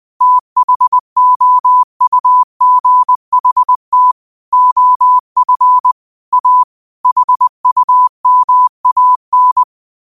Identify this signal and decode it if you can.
Morse code, heard as a keyed tone, about 20 wpm, 1000 Hz. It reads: THOUGHT OF A HUMAN
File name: qotdfiles / 20wpm